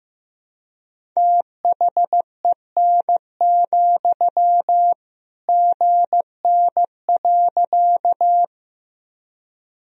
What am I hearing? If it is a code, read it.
Morse code, heard as a keyed tone, about 15 wpm, 700 Hz. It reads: THEN, GN.
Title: Sentences from Top 300 Words and Abbreviations 15wpm